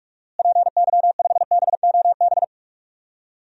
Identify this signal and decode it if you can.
Morse code, heard as a keyed tone, about 46 wpm, 700 Hz. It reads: PY5BOB
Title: DXCC-100 Call Signs - 6 Characters - Repeated 2x - 45wpm